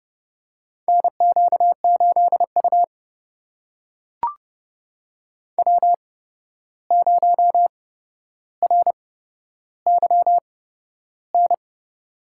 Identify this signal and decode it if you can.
Morse code, heard as a keyed tone, about 30 wpm, spaced as for 12 wpm, 700 Hz. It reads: NQ8U W0RYN
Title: USA Call Signs - 8x Character Spacing - 30wpm